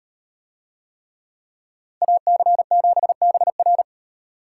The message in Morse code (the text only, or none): AC7BR